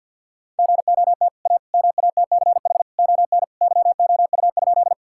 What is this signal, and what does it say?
Morse code, heard as a keyed tone, about 50 wpm, 700 Hz. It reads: GOT A MATCH ON YOU?
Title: Sentences from Top 1600 Words 50wpm